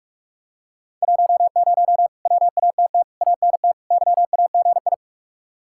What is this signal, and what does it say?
Morse code, heard as a keyed tone, about 45 wpm, 700 Hz. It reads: 10 WATT ANT YAGI